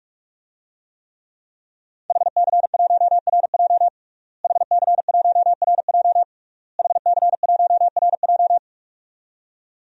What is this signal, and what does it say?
Morse code, heard as a keyed tone, about 41 wpm, 700 Hz. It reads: HC1RJ HC1RJ HC1RJ
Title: DXCC-100 Call Signs - 5 Characters - Repeated 3x - 45wpm